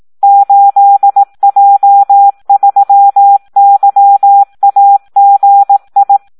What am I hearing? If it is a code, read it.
Morse code, heard as a keyed tone, about 18 wpm, 800 Hz. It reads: 8J3YAGI
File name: Audio practica CW / SD 1 18